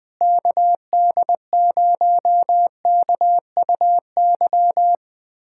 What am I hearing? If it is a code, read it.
Morse code, heard as a keyed tone, about 20 wpm, 700 Hz. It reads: KD0KUY
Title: USA Callsign DE Callsign 20wpm